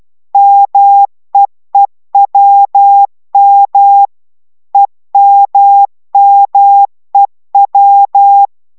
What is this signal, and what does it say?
Morse code, heard as a keyed tone, about 12 wpm, 800 Hz. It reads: MEEWM EMMEW